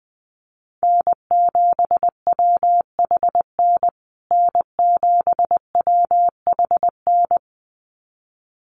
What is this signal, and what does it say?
Morse code, heard as a keyed tone, about 20 wpm, 700 Hz. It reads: N7WHN N7WHN
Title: USA Call Signs Repeated 2x 20wpm